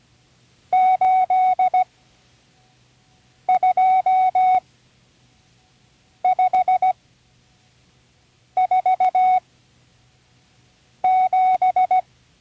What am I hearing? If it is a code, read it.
Morse code, heard as a keyed tone, about 17 wpm, spaced as for 5 wpm, 740 Hz. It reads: 82547
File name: CW Learning Programs / Learning Morse Code (from Tune in the World) / 35 answers